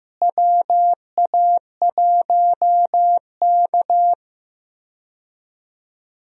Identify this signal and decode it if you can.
Morse code, heard as a keyed tone, about 15 wpm, 700 Hz. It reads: WA1K
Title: USA Call Signs - 4 Characters - Repeated 3x - 15wpm